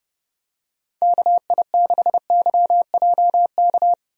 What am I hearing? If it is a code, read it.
Morse code, heard as a keyed tone, about 30 wpm, 700 Hz. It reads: KI6YJK
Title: USA Call Signs - 6 Characters - Repeated 3x - 30wpm